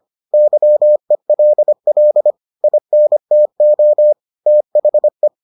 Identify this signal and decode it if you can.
Morse code, heard as a keyed tone, about 25 wpm, 600 Hz. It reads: YELL INTO THE